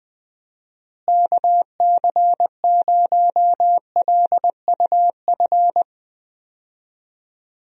Morse code, heard as a keyed tone, about 20 wpm, 700 Hz. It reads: KC0LUF
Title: USA Call Signs - 6 Characters - Repeated 2x - 20wpm